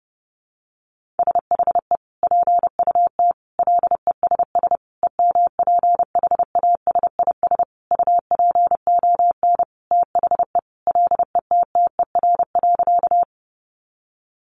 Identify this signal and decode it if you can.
Morse code, heard as a keyed tone, about 30 wpm, 700 Hz. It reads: SHE PUT LESS EMPHASIS UPON THE LETTER.
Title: Sentences from Top 1400 Words 30wpm